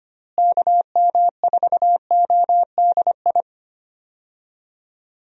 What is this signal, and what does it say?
Morse code, heard as a keyed tone, about 25 wpm, 700 Hz. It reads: KM4ODI
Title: USA Call Signs - 6 Characters - Repeated 2x - 25wpm